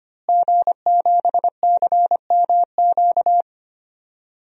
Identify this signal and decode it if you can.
Morse code, heard as a keyed tone, about 25 wpm, 700 Hz. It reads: G7CMQ
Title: DXCC-100 Call Signs - 5 Characters - Repeated 2x - 25wpm